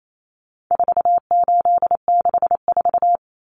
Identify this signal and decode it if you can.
Morse code, heard as a keyed tone, about 28 wpm, 700 Hz. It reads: 4864
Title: Numbers - 4-Digit 28wpm